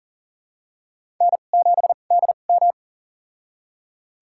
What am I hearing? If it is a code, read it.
Morse code, heard as a keyed tone, about 40 wpm, spaced as for 29 wpm, 690 Hz. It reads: N7DM
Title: USA Call Signs - 4 Characters - 2x Character Spacing - 40wpm